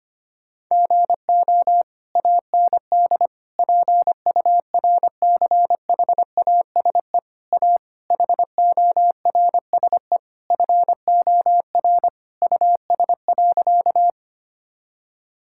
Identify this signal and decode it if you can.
Morse code, heard as a keyed tone, about 25 wpm, 700 Hz. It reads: GO AND PURCHASE A HORSE FOR US.
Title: Sentences from Top 1000 Words - Review Part 5 - 25wpm